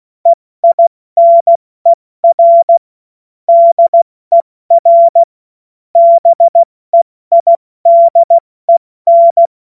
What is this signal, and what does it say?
Morse code, heard as a keyed tone, about 16 wpm, 680 Hz. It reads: EINER DER BEIDEN
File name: Morsekurs / Witze / Witz 03. Mai 2022 70 F80